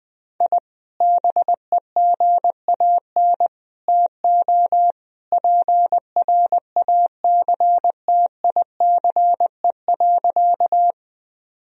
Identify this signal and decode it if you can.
Morse code, heard as a keyed tone, about 20 wpm, 700 Hz. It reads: I BEGAN TO PRACTICE.
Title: Sentences from Top 1000 Words - Review Part 4 - 20wpm